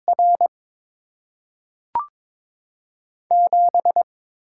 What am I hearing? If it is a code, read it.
Morse code, heard as a keyed tone, about 22 wpm, 700 Hz. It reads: R 7
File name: Top 90 QSO Elements - 6x Character Spacing - 22wpm